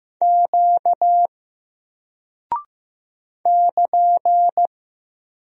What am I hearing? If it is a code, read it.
Morse code, heard as a keyed tone, about 15 wpm, 700 Hz. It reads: Q <KN>
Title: Single Character (Letter:Number:Punctuation:Prosign) 15wpm